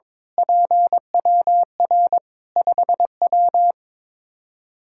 Morse code, heard as a keyed tone, about 22 wpm, 700 Hz. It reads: PWR 5W
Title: Practice QSO - Exchange 2 - Info - 22wpm